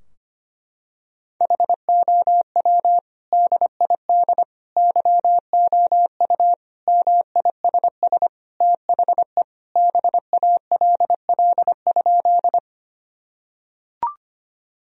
Morse code, heard as a keyed tone, about 25 wpm, 700 Hz. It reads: HOW DID YOU MISS THE BALL?
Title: Sentences from Top 1000 Words - Review Part 3 - 25wpm